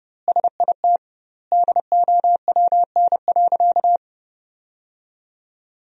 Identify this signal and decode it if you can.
Morse code, heard as a keyed tone, about 30 wpm, 700 Hz. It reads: SIT DOWN.
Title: 2x Word-Spacing - Sentences from Top 500 Words - 30wpm